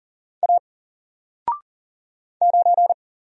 Morse code, heard as a keyed tone, about 40 wpm, 700 Hz. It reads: A 9
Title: Single Letter-Number 40wpm